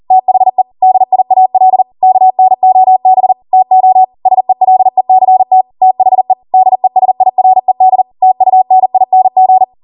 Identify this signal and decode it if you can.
Morse code, heard as a keyed tone, about 40 wpm, 750 Hz. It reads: THE DIAL KNOB TO SELECT THE DESIRED TUNING
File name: Morse / 40WPM